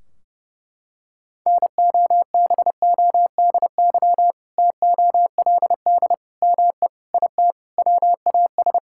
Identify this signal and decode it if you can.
Morse code, heard as a keyed tone, about 32 wpm, 700 Hz. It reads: NOBODY TOLD ME IT WAS
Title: Sentences from Top 1600 Words 30wpm